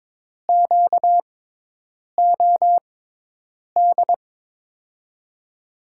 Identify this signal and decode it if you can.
Morse code, heard as a keyed tone, about 22 wpm, spaced as for 8 wpm, 700 Hz. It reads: QOD
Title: USA Call Signs - 6x Character Spacing - 22wpm